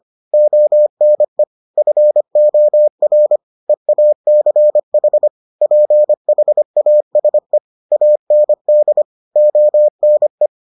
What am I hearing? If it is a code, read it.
Morse code, heard as a keyed tone, about 25 wpm, 600 Hz. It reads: ONE FOR EACH PHASE AND ONE